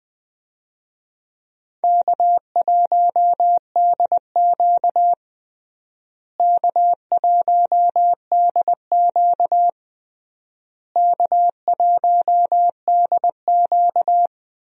Morse code, heard as a keyed tone, about 20 wpm, 700 Hz. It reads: K1DQ K1DQ K1DQ
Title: USA Call Signs - 4 Characters - Repeated 3x - 20wpm